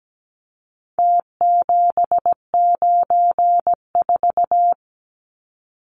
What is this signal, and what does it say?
Morse code, heard as a keyed tone, about 17 wpm, 700 Hz. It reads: T794
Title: Numbers - Contest Style - 4-Digit 17wpm